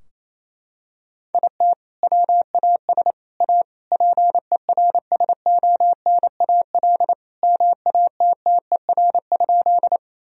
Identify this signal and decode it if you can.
Morse code, heard as a keyed tone, about 28 wpm, 700 Hz. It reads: IT WAS A PERSONAL MATTER?
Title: Sentences from Top 1000 Words - Review Part 4 - 28wpm